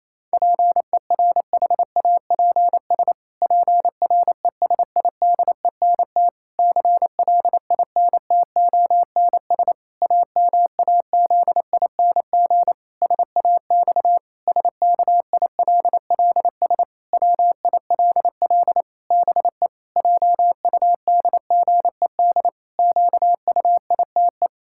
Morse code, heard as a keyed tone, about 28 wpm, 700 Hz. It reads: PERHAPS PRESIDENT CLINTONS AMAZING SAX SKILLS WILL BE JUDGED QUITE
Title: Pangram Challenge v2 - 28wpm